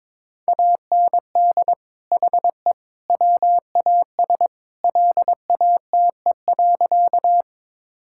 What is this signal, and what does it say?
Morse code, heard as a keyed tone, about 22 wpm, 700 Hz. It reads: AND HE WAS LATE.
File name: practice / 22wpm / Sentences with 3-Word Phrases - 22wpm